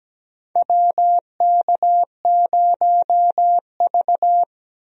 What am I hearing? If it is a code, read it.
Morse code, heard as a keyed tone, about 17 wpm, 700 Hz. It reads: WK0V
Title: USA Call Signs - 4 Characters - 17wpm